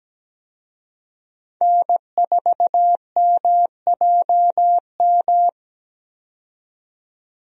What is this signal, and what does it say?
Morse code, heard as a keyed tone, about 17 wpm, 700 Hz. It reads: N4MJM
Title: USA Call Signs - 5 Characters - 17wpm